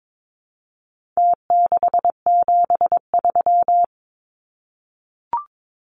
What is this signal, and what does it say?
Morse code, heard as a keyed tone, about 22 wpm, 700 Hz. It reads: T673
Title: Numbers - Contest Style - 4-Digit 22wpm